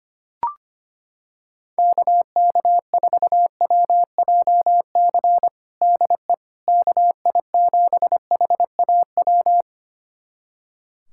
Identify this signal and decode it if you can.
Morse code, heard as a keyed tone, about 25 wpm, 700 Hz. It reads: KK4WJC DE KI7HAW
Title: USA Callsign DE Callsign - 6 Characters - 25wpm